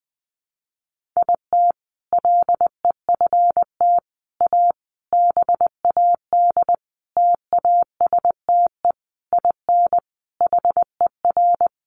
Morse code, heard as a keyed tone, about 20 wpm, 700 Hz. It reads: IT LEFT A BAD TASTE IN HER
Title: Sentences from Top 1500 Words 20wpm